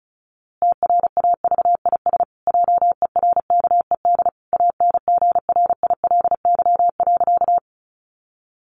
Morse code, heard as a keyed tone, about 34 wpm, 700 Hz. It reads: TRAVIS JERKED ANGRILY.
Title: A Sound Of Thunder 35wpm